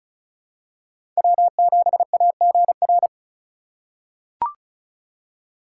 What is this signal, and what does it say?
Morse code, heard as a keyed tone, about 36 wpm, 700 Hz. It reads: W7AGR
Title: USA Call Signs - 3x Character Spacing - 35wpm